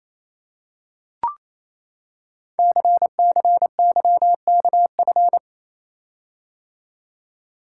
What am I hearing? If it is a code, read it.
Morse code, heard as a keyed tone, about 29 wpm, 700 Hz. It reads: CCYKF
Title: Code Groups - 5 Letters - 28wpm